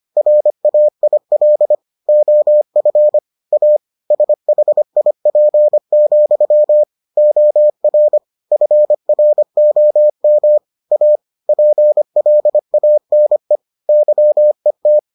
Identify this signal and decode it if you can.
Morse code, heard as a keyed tone, about 25 wpm, 600 Hz. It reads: RAIL OF A SHIP, OR FROM A PLANE YET